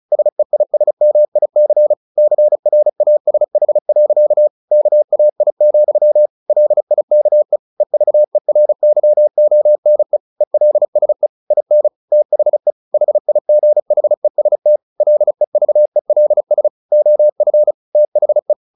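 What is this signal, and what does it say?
Morse code, heard as a keyed tone, about 34 wpm, 600 Hz. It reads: SEISMIC CRASH. KAI, LIKE EVERYONE ELSE IN THE HIGHEST LEVELS OF THE
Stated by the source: Morse practice at 35 wpm from CW med Gnister, sending Ken Follett's novel Never in English.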